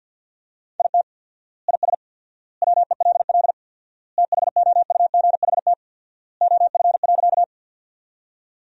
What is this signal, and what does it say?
Morse code, heard as a keyed tone, about 53 wpm, 700 Hz. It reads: IT IS WELL THOUGHT OF.
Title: 4x Word-Spacing - Sentences from Top 300 Words - 50wpm